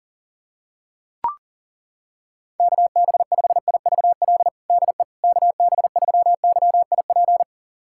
Morse code, heard as a keyed tone, about 40 wpm, 700 Hz. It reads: KB5IVL DE KB3YIP
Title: USA Callsign DE Callsign - 6 Characters - 40wpm